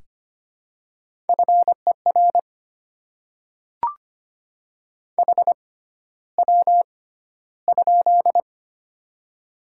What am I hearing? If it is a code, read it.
Morse code, heard as a keyed tone, about 25 wpm, spaced as for 13 wpm, 700 Hz. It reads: FER HW?
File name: Top 20 QSO Elements - 6x Character Spacing - 25wpm